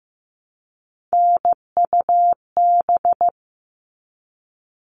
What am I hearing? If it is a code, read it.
Morse code, heard as a keyed tone, about 15 wpm, 700 Hz. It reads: NUB
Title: All Cut Numbers - 3-Digit 15wpm